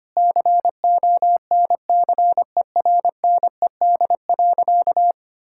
Morse code, heard as a keyed tone, about 25 wpm, 700 Hz. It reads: CONCERNED.
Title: Sentences from Top 1000 Words - Review Part 6 - 25wpm